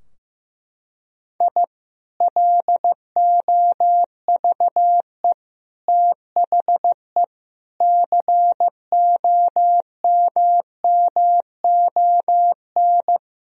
Morse code, heard as a keyed tone, about 15 wpm, 700 Hz. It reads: I LOVE THE COMMON
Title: Sentences from Top 1000 Words - Review Part 3 - 15wpm